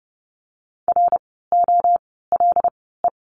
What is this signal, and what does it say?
Morse code, heard as a keyed tone, about 30 wpm, 700 Hz. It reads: R O L E
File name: Lesson 16f - Letter c - Words 30wpm